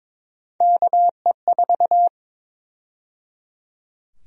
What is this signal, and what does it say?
Morse code, heard as a keyed tone, about 22 wpm, 700 Hz. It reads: KE4
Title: USA Call Sign Prefixes - 22wpm